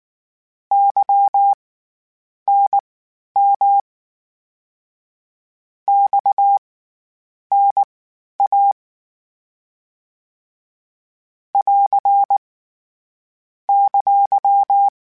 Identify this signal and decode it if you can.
Morse code, heard as a keyed tone, about 19 wpm, spaced as for 9 wpm, 800 Hz. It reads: YNM XNA <AR> !
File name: CW Comparisons / CW Comparisons-mp3s / CW Comparisons Broken Characters @18